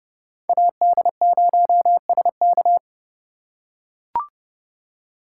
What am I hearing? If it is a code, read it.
Morse code, heard as a keyed tone, about 30 wpm, 700 Hz. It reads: AD0SK
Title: USA Call Signs - 5 Characters - 3x Character Spacing - 30wpm